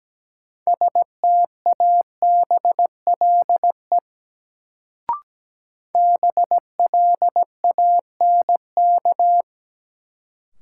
Morse code, heard as a keyed tone, about 17 wpm, 700 Hz. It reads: STABLE BLANK
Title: Sentences from Top 1500 Words - New Word Drill Alt - 17wpm